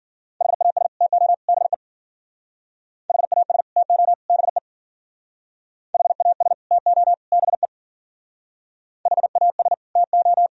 Morse code, heard as a keyed tone, about 53 wpm, 700 Hz. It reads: HAS TO BE HAS TO BE HAS TO BE HAS TO